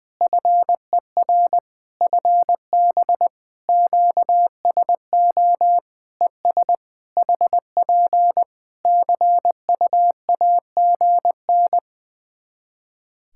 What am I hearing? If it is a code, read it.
Morse code, heard as a keyed tone, about 21 wpm, 700 Hz. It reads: FER FB QSO ES HP CUAGN
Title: Practice QSO - Exchange 4 - Ending - 20wpm